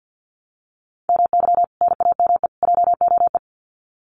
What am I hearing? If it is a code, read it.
Morse code, heard as a keyed tone, about 50 wpm, 700 Hz. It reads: MY NAME JOE
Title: Common QSO Phrases 50wpm